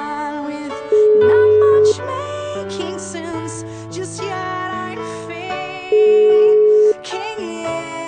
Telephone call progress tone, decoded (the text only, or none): ringback tone